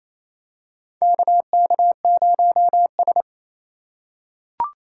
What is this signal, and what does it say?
Morse code, heard as a keyed tone, about 28 wpm, 700 Hz. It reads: KK0S
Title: USA Call Signs - 4 Characters - 6x Character Spacing - 28wpm